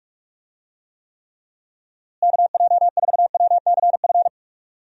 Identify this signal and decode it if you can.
Morse code, heard as a keyed tone, about 47 wpm, 700 Hz. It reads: KJ4WCF
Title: USA Call Signs - 6 Characters - Repeated 3x - 45wpm